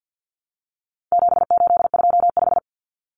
Morse code, heard as a keyed tone, about 50 wpm, 700 Hz. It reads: 7825
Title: Numbers - 4-Digit 50wpm